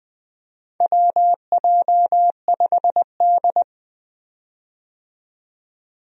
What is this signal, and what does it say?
Morse code, heard as a keyed tone, about 20 wpm, 700 Hz. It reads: WJ5D
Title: USA Call Signs - 4 Characters - 20wpm